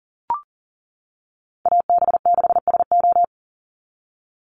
Morse code, heard as a keyed tone, about 40 wpm, 700 Hz. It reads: AB6SO
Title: NA SOTA Chaser Callsigns 40wpm